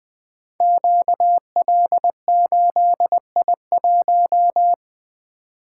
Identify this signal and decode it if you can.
Morse code, heard as a keyed tone, about 20 wpm, 700 Hz. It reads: QL8I1
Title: Code Groups - 5 Letter:Numbers - 20wpm